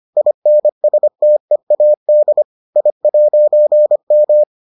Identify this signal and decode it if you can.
Morse code, heard as a keyed tone, about 25 wpm, 600 Hz. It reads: INSTEAD I'M